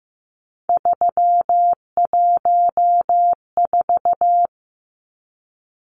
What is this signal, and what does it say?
Morse code, heard as a keyed tone, about 15 wpm, 700 Hz. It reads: 314
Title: Numbers - 3-Digit 15wpm